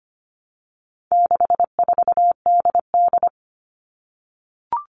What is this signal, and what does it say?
Morse code, heard as a keyed tone, about 25 wpm, 700 Hz. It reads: 64DD
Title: All Cut Numbers - 4-Digit 25wpm